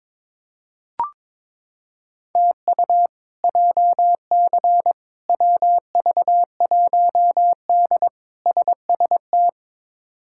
Morse code, heard as a keyed tone, about 22 wpm, 700 Hz. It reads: TU JC WV1D SST
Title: K1USN SST QSO Practice - 22wpm